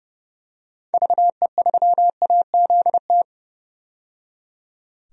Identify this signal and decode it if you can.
Morse code, heard as a keyed tone, about 30 wpm, 700 Hz. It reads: VE3AZT
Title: DXCC-100 Call Signs - 6 Characters - 30wpm